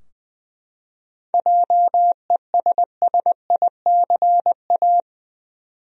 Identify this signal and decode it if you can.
Morse code, heard as a keyed tone, about 20 wpm, 700 Hz. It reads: JESSICA
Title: USA Common Names - 20wpm